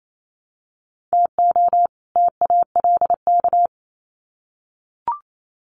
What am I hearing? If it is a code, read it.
Morse code, heard as a keyed tone, about 28 wpm, 700 Hz. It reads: TO TALK
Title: Sets of 2 Words 28wpm